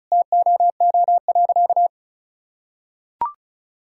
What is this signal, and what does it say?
Morse code, heard as a keyed tone, about 36 wpm, 700 Hz. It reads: TOO.
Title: Sentences from Top 1000 Words - Review Part 6 - 35wpm